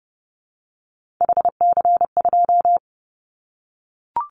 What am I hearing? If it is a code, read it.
Morse code, heard as a keyed tone, about 30 wpm, 700 Hz. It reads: HC2
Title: Callsign Prefixes for Top 100 DXCC - 30wpm